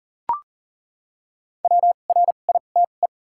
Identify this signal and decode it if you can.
Morse code, heard as a keyed tone, about 40 wpm, 700 Hz. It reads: W R I T E
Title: 2x Character Spacing - Top 200 Words - 40wpm